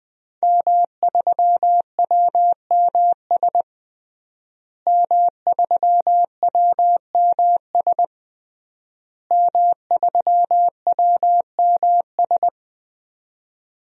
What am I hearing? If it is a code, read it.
Morse code, heard as a keyed tone, about 20 wpm, 700 Hz. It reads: M3WMS M3WMS M3WMS
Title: DXCC-100 Call Signs - 5 Characters - Repeated 3x - 20wpm